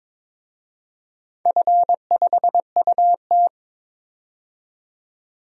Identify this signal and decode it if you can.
Morse code, heard as a keyed tone, about 22 wpm, 700 Hz. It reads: F5UT
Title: DXCC-100 Call Signs - 4 Characters - 22wpm